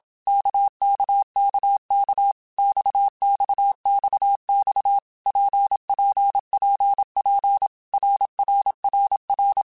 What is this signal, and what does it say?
Morse code, heard as a keyed tone, about 27 wpm, 800 Hz. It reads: KKKK XXXX PPPP RRRR
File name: Basic Letter Groups By7Groups / RKPX / RKPX@25